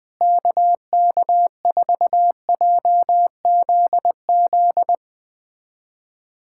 Morse code, heard as a keyed tone, about 20 wpm, 700 Hz. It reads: KK4JZZ
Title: USA Callsign DE Callsign 20wpm